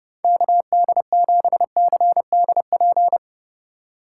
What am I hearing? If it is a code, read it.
Morse code, heard as a keyed tone, about 28 wpm, 700 Hz. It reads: KD7CDP
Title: Speed Racing - USA 2x3 Call Signs 30wpm